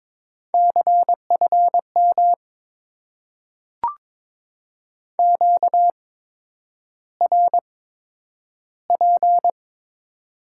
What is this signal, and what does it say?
Morse code, heard as a keyed tone, about 22 wpm, spaced as for 9 wpm, 700 Hz. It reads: CFM QRP
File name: Top 90 QSO Elements - 8x Character Spacing - 22wpm